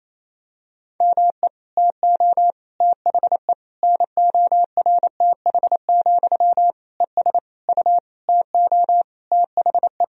Morse code, heard as a keyed tone, about 28 wpm, 700 Hz. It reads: ME TO THE NORTH, ES U TO THE
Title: Sentences from Top 400 Words and Abbreviations 28wpm